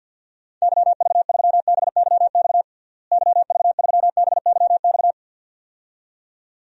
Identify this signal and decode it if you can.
Morse code, heard as a keyed tone, about 47 wpm, 700 Hz. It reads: YV3BYX YV3BYX
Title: DXCC-100 Call Signs - 6 Characters - Repeated 2x - 50wpm